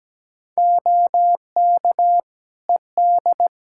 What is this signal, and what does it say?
Morse code, heard as a keyed tone, about 17 wpm, 700 Hz. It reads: OK ED
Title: Practice QSO - Exchange 3 - More Info - 17wpm